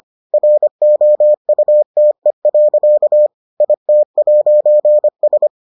Morse code, heard as a keyed tone, about 25 wpm, 600 Hz. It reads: ROUTE. IT'S